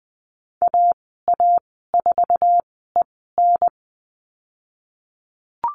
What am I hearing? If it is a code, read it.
Morse code, heard as a keyed tone, about 20 wpm, spaced as for 15 wpm, 700 Hz. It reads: AA4EN
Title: 076 Lesson 13g - Letter L - Call Signs 20wpm